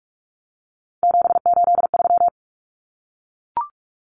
Morse code, heard as a keyed tone, about 44 wpm, 700 Hz. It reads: 783
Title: Common Cut Numbers - 3-Digit 45wpm